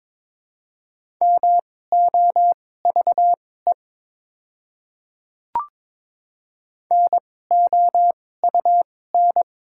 Morse code, heard as a keyed tone, about 22 wpm, spaced as for 16 wpm, 700 Hz. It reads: MOVE NOUN
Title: 2x Character Spacing - Top 500 Words - 22wpm